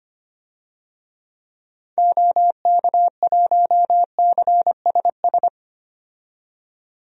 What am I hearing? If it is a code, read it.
Morse code, heard as a keyed tone, about 25 wpm, 700 Hz. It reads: OK1CSS
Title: DXCC-100 Call Signs - 6 Characters - 25wpm